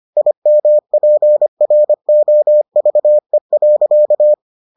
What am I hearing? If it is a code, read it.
Morse code, heard as a keyed tone, about 25 wpm, 600 Hz. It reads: IMPROVE.